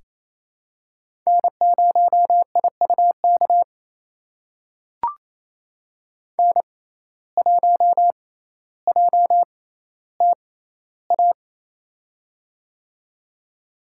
Morse code, heard as a keyed tone, about 28 wpm, spaced as for 14 wpm, 700 Hz. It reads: N0IUK N1JTA
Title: USA Call Signs - 5 Characters - 6x Character Spacing - 28wpm